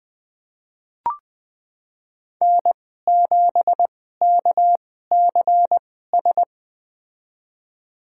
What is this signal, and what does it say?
Morse code, heard as a keyed tone, about 20 wpm, spaced as for 14 wpm, 700 Hz. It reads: N7KCS
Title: USA Call Signs - 5 Characters - 2x Character Spacing - 20wpm